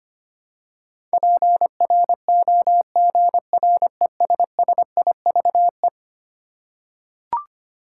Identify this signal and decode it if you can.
Morse code, heard as a keyed tone, about 25 wpm, 700 Hz. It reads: PROGRESSIVE
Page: morse-code-ninja-practice-25wpm